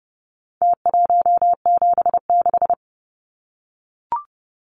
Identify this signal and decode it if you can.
Morse code, heard as a keyed tone, about 30 wpm, 700 Hz. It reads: T176
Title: Numbers - Contest Style - 4-Digit 30wpm